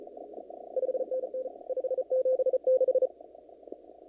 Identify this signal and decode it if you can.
Morse code, heard as a keyed tone, about 34 wpm, 520 Hz. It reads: 5NN 576
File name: CW / kt5j wpx cw-op-k5pi